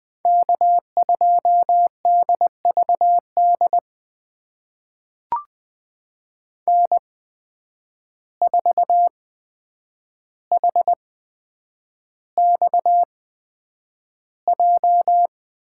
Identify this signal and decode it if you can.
Morse code, heard as a keyed tone, about 20 wpm, spaced as for 9 wpm, 700 Hz. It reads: K2DVD N4HXJ
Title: USA Call Signs - 5 Characters - 8x Character Spacing - 20wpm